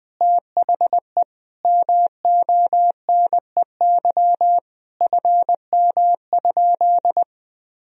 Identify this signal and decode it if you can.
Morse code, heard as a keyed tone, about 20 wpm, 700 Hz. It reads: THE MONEY FM?
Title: Sentences from Top 400 Words and Abbreviations 20wpm